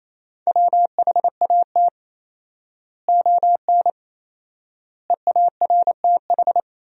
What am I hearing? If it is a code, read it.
Morse code, heard as a keyed tone, about 28 wpm, 700 Hz. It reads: WHAT ON EARTH
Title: 4x Word-Spacing - Sentences from Top 200 Words - 28wpm